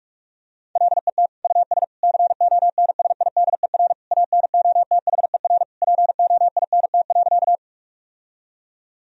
Morse code, heard as a keyed tone, about 46 wpm, 700 Hz. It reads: LET US CONSIDER ANOTHER POINT.
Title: Sentences from Top 1000 Words - Review Part 4 - 45wpm